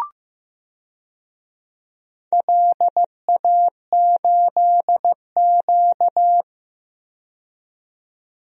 Morse code, heard as a keyed tone, about 15 wpm, 700 Hz. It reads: LA8Q
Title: DXCC-100 Call Signs - 4 Characters - 15wpm